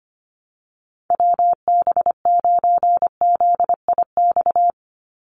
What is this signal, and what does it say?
Morse code, heard as a keyed tone, about 25 wpm, 700 Hz. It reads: WB9ZIX
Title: USA Call Signs 25wpm